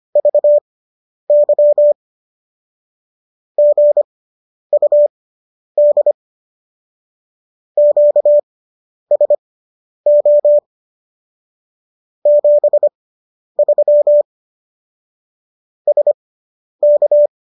Morse code, heard as a keyed tone, about 25 wpm, spaced as for 10 wpm, 590 Hz. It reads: VY GUD QSO 73 SK